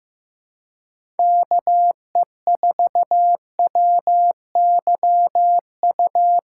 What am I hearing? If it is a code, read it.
Morse code, heard as a keyed tone, about 15 wpm, 700 Hz. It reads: KE4WYU
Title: USA Callsign DE Callsign - 6 Characters - 15wpm